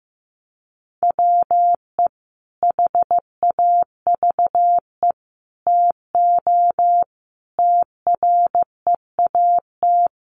Morse code, heard as a keyed tone, about 15 wpm, 700 Hz. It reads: WE HAVE TO TREAT
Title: Sentences from Top 1300 Words 15wpm